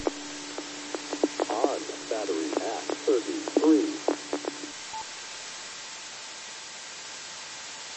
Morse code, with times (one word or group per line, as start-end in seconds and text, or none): none